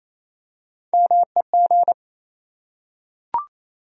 Morse code, heard as a keyed tone, about 27 wpm, 700 Hz. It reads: MEG
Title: USA Common Names - Max 3 Letters - 28wpm